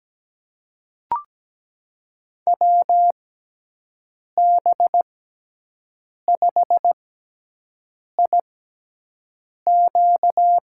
Morse code, heard as a keyed tone, about 17 wpm, spaced as for 6 wpm, 700 Hz. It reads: WB5IQ
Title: USA Call Signs - 6 Characters - 6x Character Spacing - 17wpm